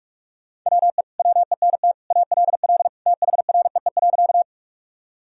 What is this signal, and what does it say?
Morse code, heard as a keyed tone, about 46 wpm, 700 Hz. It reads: WE WENT ALL THREE.
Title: Sentences from Top 1000 Words - Review Part 1 - 45wpm